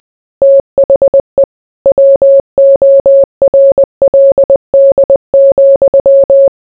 Morse code, heard as a keyed tone, about 20 wpm, 550 Hz. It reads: THE WORLD,